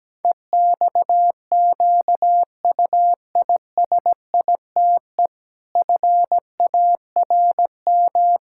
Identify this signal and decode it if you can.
Morse code, heard as a keyed tone, about 17 wpm, 700 Hz. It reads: EXQUISITE FARM
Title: Pangram Challenge v2 - 17wpm